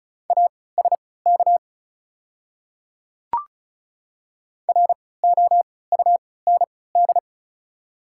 Morse code, heard as a keyed tone, about 36 wpm, spaced as for 20 wpm, 700 Hz. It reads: ASK ROUND
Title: 3x Character Spacing - Top 200 Words - 35wpm